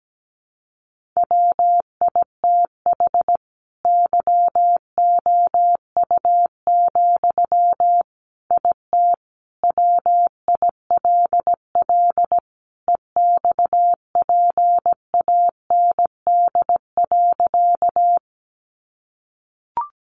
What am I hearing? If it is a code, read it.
Morse code, heard as a keyed tone, about 17 wpm, 700 Hz. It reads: WITH YOU, IT WILL EXPAND.
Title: Sentences from Top 1400 Words 17wpm